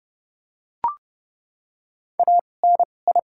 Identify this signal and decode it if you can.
Morse code, heard as a keyed tone, about 30 wpm, spaced as for 22 wpm, 700 Hz. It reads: ANI
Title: 2x Character Spacing - Top 500 Words - 30wpm